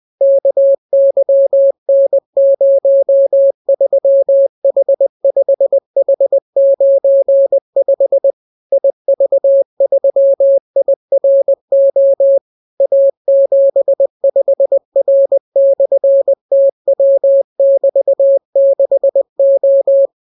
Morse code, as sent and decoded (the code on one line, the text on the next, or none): -.- -.-- -. ----- ...-- .... ..... .... ----. ..... / .. ...- ...-- .. .-. --- / .- --... ..... .-. -..-. - .-- -...- -.... ---
KYN03H5H95 IV3IRO A75R/TW<BT>6O